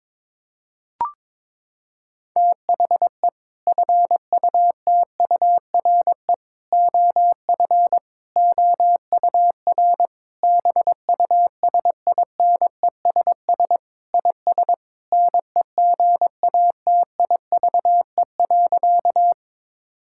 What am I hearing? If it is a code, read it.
Morse code, heard as a keyed tone, about 22 wpm, 700 Hz. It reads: THE FUTURE OF OUR BUSINESS IS NEGATIVE.
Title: Sentences from Top 1000 Words - Review Part 6 - 22wpm